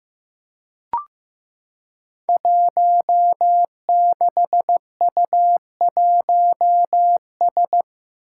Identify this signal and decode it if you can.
Morse code, heard as a keyed tone, about 15 wpm, 700 Hz. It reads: 16U1S
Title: Five Challenging Letter-Number Combinations 15wpm